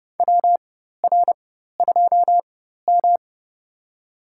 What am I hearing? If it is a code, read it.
Morse code, heard as a keyed tone, about 30 wpm, spaced as for 14 wpm, 700 Hz. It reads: WR2M
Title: USA Call Signs - 4x Character Spacing - 30wpm